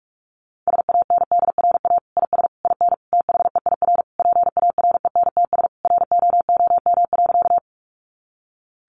Morse code, heard as a keyed tone, about 44 wpm, 700 Hz. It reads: SANDRA IS IN THEIR PARENTS ROOM.
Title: Easy Phrases 45wpm